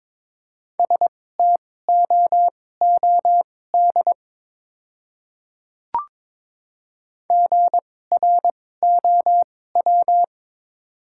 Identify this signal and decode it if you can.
Morse code, heard as a keyed tone, about 22 wpm, spaced as for 16 wpm, 700 Hz. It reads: STOOD GROW
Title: 2x Character Spacing - Top 500 Words - 22wpm